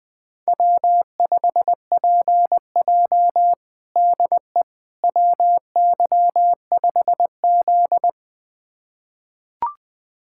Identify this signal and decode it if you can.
Morse code, heard as a keyed tone, about 20 wpm, 700 Hz. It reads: W5PJ DE WY5Z
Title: USA Callsign DE Callsign - 4 Characters - 20wpm